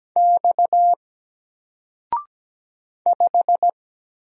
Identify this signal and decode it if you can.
Morse code, heard as a keyed tone, about 17 wpm, 700 Hz. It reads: X 5
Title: Single Letter-Number Alt 17wpm